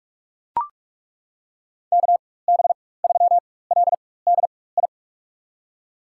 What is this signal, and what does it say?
Morse code, heard as a keyed tone, about 43 wpm, spaced as for 21 wpm, 700 Hz. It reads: KB3LDI
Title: USA Call Signs - 4x Character Spacing - 45wpm